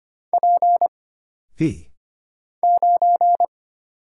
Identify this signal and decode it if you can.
Morse code, heard as a keyed tone, about 25 wpm, 700 Hz. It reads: P 9
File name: Single Character (Letter:Number:Punctuation:Prosign) - Mind-Melt - 25wpm